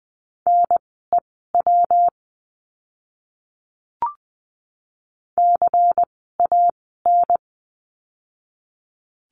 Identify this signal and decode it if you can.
Morse code, heard as a keyed tone, about 20 wpm, spaced as for 14 wpm, 700 Hz. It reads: NEW CAN
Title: 139 Lesson 23f - Letter f - Words 20wpm